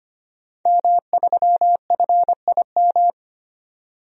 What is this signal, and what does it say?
Morse code, heard as a keyed tone, about 25 wpm, 700 Hz. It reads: M3FIM
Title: DXCC-100 Call Signs - 5 Characters - 25wpm